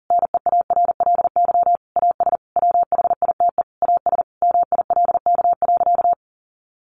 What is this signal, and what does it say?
Morse code, heard as a keyed tone, about 40 wpm, 700 Hz. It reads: NEARLY AS WHITE AS MILK.
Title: Sentences from Top 1500 Words 40wpm